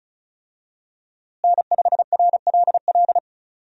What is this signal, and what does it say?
Morse code, heard as a keyed tone, about 36 wpm, 700 Hz. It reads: N5RLL
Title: USA Call Signs - 5 Characters - Repeated 2x - 35wpm